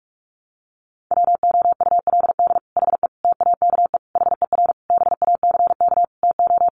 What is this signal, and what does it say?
Morse code, heard as a keyed tone, about 44 wpm, 700 Hz. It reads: WOULD HE TAKE HER BACK TO
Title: Sentences from Top 1500 Words 45wpm